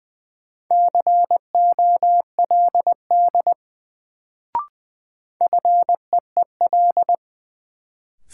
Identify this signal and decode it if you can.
Morse code, heard as a keyed tone, about 20 wpm, 700 Hz. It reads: COLD FEEL
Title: Sets of 5 Words - New Word Drill - 20wpm